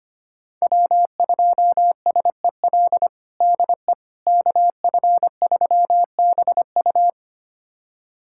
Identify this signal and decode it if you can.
Morse code, heard as a keyed tone, about 25 wpm, 700 Hz. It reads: W2SEL DE KF3BU
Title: USA Callsign DE Callsign - 5 Characters - 25wpm